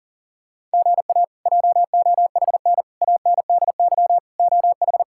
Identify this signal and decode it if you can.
Morse code, heard as a keyed tone, about 37 wpm, 690 Hz. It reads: GA JOHN ANDY OH
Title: K1USN SST QSO Practice - 40wpm